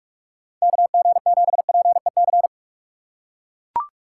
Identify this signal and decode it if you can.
Morse code, heard as a keyed tone, about 44 wpm, 690 Hz. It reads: KG7PEC